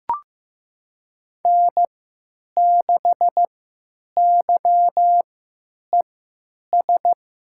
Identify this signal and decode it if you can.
Morse code, heard as a keyed tone, about 15 wpm, spaced as for 9 wpm, 700 Hz. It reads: N6YES
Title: USA Call Signs - 3x Character Spacing - 15wpm